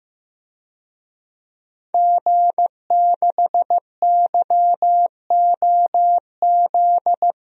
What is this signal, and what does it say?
Morse code, heard as a keyed tone, about 15 wpm, 700 Hz. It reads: G6YOZ
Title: DXCC-100 Call Signs - 5 Characters - Repeated 2x - 15wpm